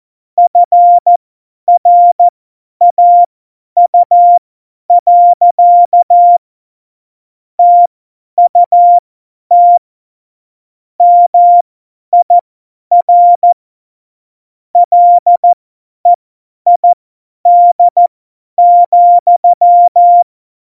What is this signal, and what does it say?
Morse code, heard as a keyed tone, about 14 wpm, spaced as for 10 wpm, 700 Hz. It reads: FRAU. TUT MIR LEID,
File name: Morsekurs / Witze / Witz 19 Mai 2020, 50 F70